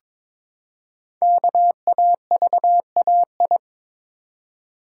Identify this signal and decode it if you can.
Morse code, heard as a keyed tone, about 22 wpm, 700 Hz. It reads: KAVAI
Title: Code Groups - 5 Letters - 22wpm